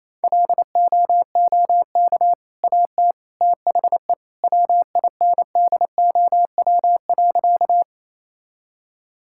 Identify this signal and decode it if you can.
Morse code, heard as a keyed tone, about 28 wpm, 700 Hz. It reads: LOOK AT THE WINDOW.
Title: Sentences from Top 1600 Words 28wpm